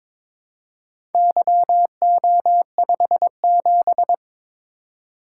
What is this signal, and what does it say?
Morse code, heard as a keyed tone, about 22 wpm, 700 Hz. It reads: YO57
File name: Four Letter-Number Combinations - 22wpm